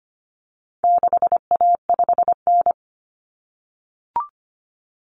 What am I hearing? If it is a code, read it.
Morse code, heard as a keyed tone, about 25 wpm, 700 Hz. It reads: 6A5N
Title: Common Cut Numbers - 4-Digit 25wpm